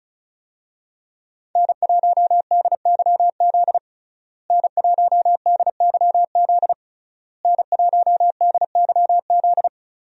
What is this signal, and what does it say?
Morse code, heard as a keyed tone, about 34 wpm, 700 Hz. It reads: N1DYZ N1DYZ N1DYZ
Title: USA Call Signs - 5 Characters - Repeated 3x - 35wpm